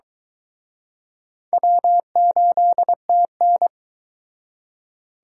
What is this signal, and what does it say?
Morse code, heard as a keyed tone, about 22 wpm, 700 Hz. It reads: W8TN